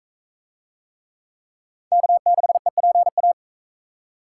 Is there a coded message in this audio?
Yes: Morse code from 1.9 to 3.3 s, heard as a keyed tone, about 40 wpm, 700 Hz. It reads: K6EPA